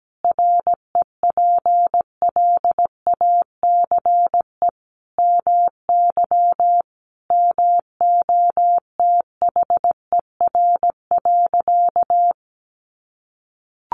Morse code, heard as a keyed tone, about 17 wpm, 700 Hz. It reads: REPLACE MY MOTHER.
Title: Sentences from Top 1400 Words 17wpm